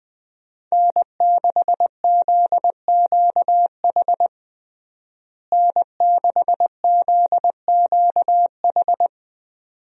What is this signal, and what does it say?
Morse code, heard as a keyed tone, about 20 wpm, 700 Hz. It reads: N6ZQH N6ZQH
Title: USA Call Signs - 5 Characters - Repeated 2x - 20wpm